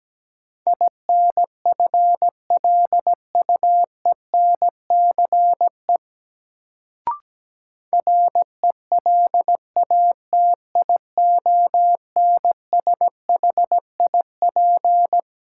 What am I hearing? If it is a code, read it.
Morse code, heard as a keyed tone, about 17 wpm, 700 Hz. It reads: INFLUENCE RELATIONSHIP
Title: Sentences from Top 1100 Words - New Word Drill - 17wpm